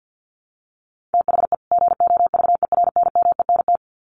Morse code, heard as a keyed tone, about 50 wpm, 700 Hz. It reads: THE GOVERNMENT
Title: Sets of 2 Words 50wpm